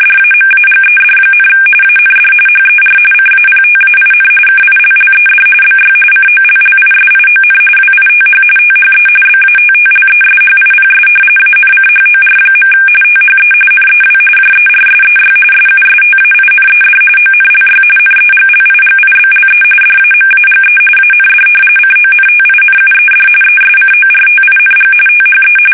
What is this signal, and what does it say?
RTTY, about 45.45 baud. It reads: DAMAGED, AND HER CAPTAIN KILLED BY COLLISION WITH THE BATTLESHIPS ANCHOR. THE NEW JERSEY SUSTAINED ONLY NEGLIGIBLE DAMAGE AND CONTINUED ESCORTING THE
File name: fsk850 / battleshipnj